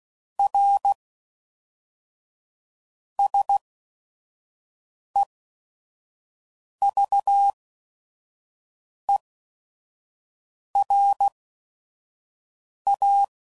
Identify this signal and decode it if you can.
Morse code, heard as a keyed tone, about 16 wpm, spaced as for 5 wpm, 800 Hz. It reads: R SEVERA
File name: CW Code Groups @5WPM / QSOs / kc7haa